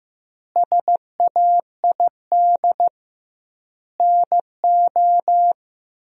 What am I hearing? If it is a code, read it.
Morse code, heard as a keyed tone, about 14 wpm, 700 Hz. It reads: SAID NO
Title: 2x Word-Spacing - Sentences from Top 100 Words - 15wpm